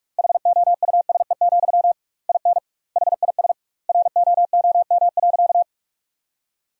Morse code, heard as a keyed tone, about 44 wpm, 700 Hz. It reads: HOUSE, IN HIS ROOM.
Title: 2x Word-Spacing - Sentences from Top 300 Words - 45wpm